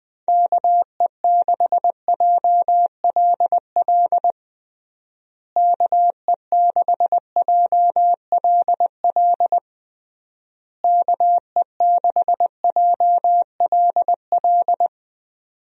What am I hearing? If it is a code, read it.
Morse code, heard as a keyed tone, about 21 wpm, 700 Hz. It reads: KE6JLL KE6JLL KE6JLL
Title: USA Call Signs - 6 Characters - Repeated 3x - 20wpm